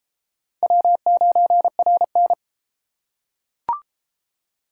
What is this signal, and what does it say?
Morse code, heard as a keyed tone, about 31 wpm, 700 Hz. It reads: W9RN